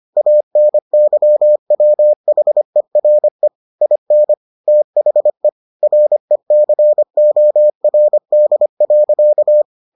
Morse code, heard as a keyed tone, about 25 wpm, 600 Hz. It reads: ANYWHERE IN THE RECORD.